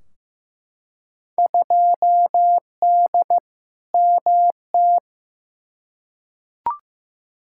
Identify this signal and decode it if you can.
Morse code, heard as a keyed tone, about 15 wpm, 700 Hz. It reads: 2D MT
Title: Field Day Exchanges Encore - 15wpm